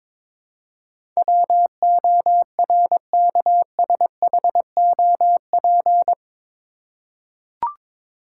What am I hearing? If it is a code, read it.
Morse code, heard as a keyed tone, about 22 wpm, 700 Hz. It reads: WORKSHOP
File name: practice / 22wpm / Eight Letter Words - 22wpm